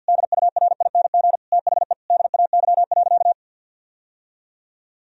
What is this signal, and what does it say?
Morse code, heard as a keyed tone, about 50 wpm, 700 Hz. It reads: DURING THE DAY.
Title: Sentences from Top 1000 Words - Review Part 3 - 50wpm